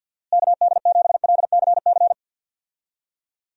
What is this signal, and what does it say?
Morse code, heard as a keyed tone, about 50 wpm, 700 Hz. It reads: KD7LCC
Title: USA Call Signs - 6 Characters - Repeated 2x - 50wpm